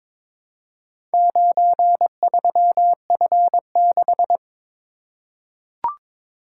Morse code, heard as a keyed tone, about 22 wpm, 700 Hz. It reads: 93F6
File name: practice / 22wpm / Four Letter-Number Combinations - 22wpm